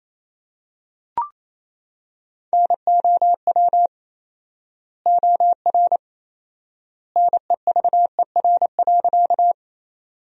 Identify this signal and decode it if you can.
Morse code, heard as a keyed tone, about 28 wpm, 700 Hz. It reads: NOW OR NEVER.
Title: 4x Word-Spacing - Sentences from Top 300 Words - 28wpm